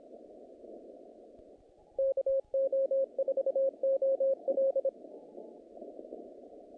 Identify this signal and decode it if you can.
Morse code, heard as a keyed tone, about 26 wpm, 540 Hz. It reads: KO4OL
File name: contests / naqp-cw-k5tr-op-k5pi- -2100